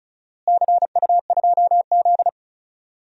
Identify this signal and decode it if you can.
Morse code, heard as a keyed tone, about 34 wpm, 700 Hz. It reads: CU2Z
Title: DXCC-100 Call Signs - 4 Characters - Repeated 2x - 35wpm